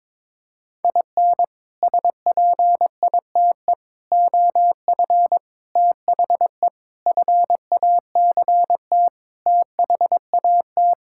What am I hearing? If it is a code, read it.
Morse code, heard as a keyed tone, about 22 wpm, 700 Hz. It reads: IN SPITE OF THE FACT THAT
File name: practice / 22wpm / Sets of 6 Words - 22wpm